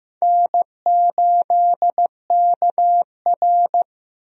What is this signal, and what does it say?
Morse code, heard as a keyed tone, about 15 wpm, 700 Hz. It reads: N8KR
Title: USA Call Signs - 4 Characters - 15wpm